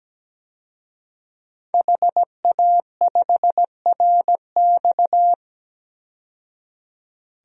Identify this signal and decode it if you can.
Morse code, heard as a keyed tone, about 17 wpm, 700 Hz. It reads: HA5RX
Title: DXCC-100 Call Signs - 5 Characters - 17wpm